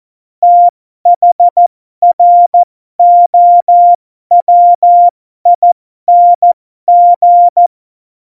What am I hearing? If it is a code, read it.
Morse code, heard as a keyed tone, about 14 wpm, spaced as for 12 wpm, 700 Hz. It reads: THROWING
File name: Morsekurs / Witze